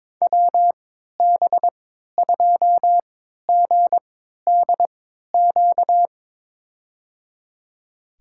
Morse code, heard as a keyed tone, about 22 wpm, spaced as for 12 wpm, 700 Hz. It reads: WB2GDQ
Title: USA Call Signs - 6 Characters - 3x Character Spacing - 22wpm